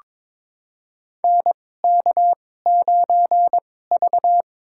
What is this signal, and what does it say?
Morse code, heard as a keyed tone, about 22 wpm, spaced as for 16 wpm, 700 Hz. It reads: NK9V
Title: USA Call Signs - 4 Characters - 2x Character Spacing - 22wpm